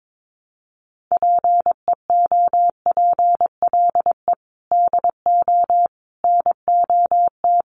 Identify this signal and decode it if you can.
Morse code, heard as a keyed tone, about 22 wpm, 700 Hz. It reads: PEOPLE DO NOT
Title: Sets of 5 Words Encore2 22wpm